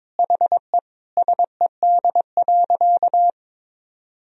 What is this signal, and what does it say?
Morse code, heard as a keyed tone, about 22 wpm, 700 Hz. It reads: HE SED.
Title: Sentences from Top 200 Words and Abbreviations 22wpm